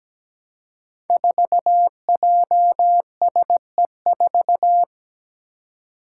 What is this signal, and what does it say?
Morse code, heard as a keyed tone, about 17 wpm, 700 Hz. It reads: 4JSE4
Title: Five Challenging Letter-Number Combinations 17wpm